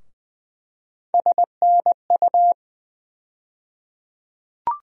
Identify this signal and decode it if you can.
Morse code, heard as a keyed tone, about 20 wpm, 700 Hz. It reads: SNU
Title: USA Call Sign Postfixes - 3 Letters 20wpm